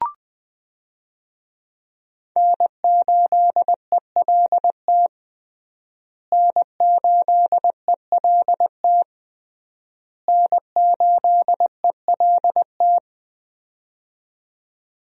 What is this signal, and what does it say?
Morse code, heard as a keyed tone, about 21 wpm, 700 Hz. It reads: N8ELT N8ELT N8ELT
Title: USA Call Signs - 5 Characters - Repeated 3x - 20wpm